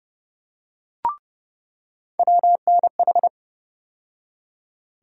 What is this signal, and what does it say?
Morse code, heard as a keyed tone, about 30 wpm, 700 Hz. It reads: WNH
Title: USA Call Sign Postfixes - 3 Letters 30wpm